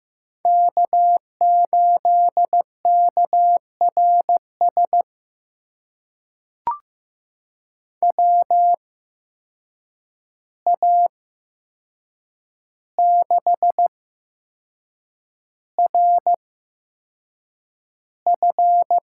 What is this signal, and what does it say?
Morse code, heard as a keyed tone, about 15 wpm, spaced as for 6 wpm, 700 Hz. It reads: K8KRS WA6RF
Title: USA Call Signs - 5 Characters - 8x Character Spacing - 15wpm